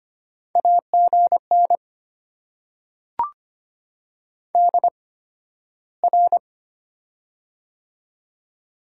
Morse code, heard as a keyed tone, about 25 wpm, spaced as for 13 wpm, 700 Hz. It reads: AGN DR
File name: Top 90 QSO Elements - 8x Character Spacing - 25wpm